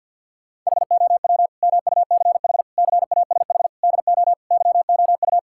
Morse code, heard as a keyed tone, about 50 wpm, 690 Hz. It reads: HOW MUCH CASH DO YOU
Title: Sentences from Top 1000 Words - Review Part 5 - 50wpm